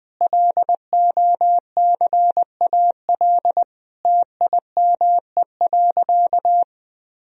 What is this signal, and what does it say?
Morse code, heard as a keyed tone, about 21 wpm, 700 Hz. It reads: LOCAL TIME.
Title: Sentences from Top 1000 Words - Review Part 3 - 20wpm